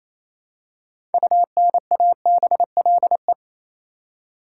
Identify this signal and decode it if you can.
Morse code, heard as a keyed tone, about 28 wpm, 700 Hz. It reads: UNABLE
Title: Six Letter Words - 28wpm